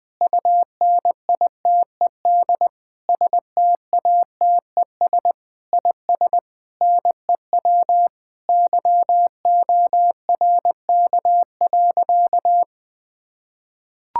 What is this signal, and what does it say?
Morse code, heard as a keyed tone, about 20 wpm, 700 Hz. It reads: UNITED STATES IS NEW YORK.
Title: Sentences with 5-Word Phrases - 20wpm